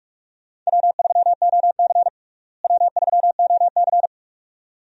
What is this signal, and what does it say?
Morse code, heard as a keyed tone, about 46 wpm, 700 Hz. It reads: W3OC W3OC
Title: USA Call Signs - 4 Characters - Repeated 2x - 45wpm